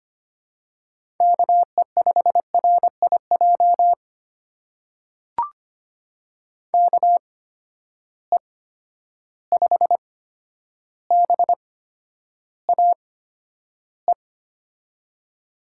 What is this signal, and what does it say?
Morse code, heard as a keyed tone, about 25 wpm, spaced as for 11 wpm, 700 Hz. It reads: KE5RIJ KE5BAE
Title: USA Call Signs - 6 Characters - 8x Character Spacing - 25wpm